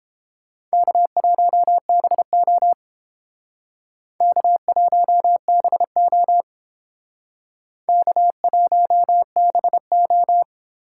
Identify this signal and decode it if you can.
Morse code, heard as a keyed tone, about 28 wpm, 700 Hz. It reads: K1BO K1BO K1BO